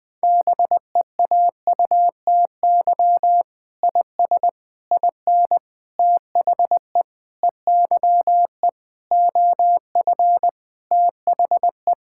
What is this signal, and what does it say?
Morse code, heard as a keyed tone, about 21 wpm, 700 Hz. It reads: BEAUTY IS IN THE EYE OF THE
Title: Sentences with 6-Word Phrases - Spoken Beginning - 20wpm